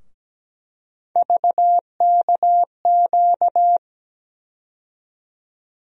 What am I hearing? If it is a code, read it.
Morse code, heard as a keyed tone, about 17 wpm, 700 Hz. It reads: VKQ
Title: USA Call Sign Postfixes - 3 Letters 17wpm